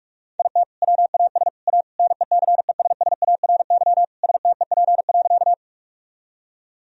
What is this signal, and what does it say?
Morse code, heard as a keyed tone, about 41 wpm, 700 Hz. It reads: IT WAS A NECESSARY STEP.
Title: Sentences from Top 1000 Words - Review Part 2 - 45wpm